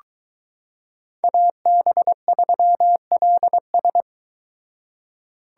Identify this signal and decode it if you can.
Morse code, heard as a keyed tone, about 22 wpm, 700 Hz. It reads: AB3LS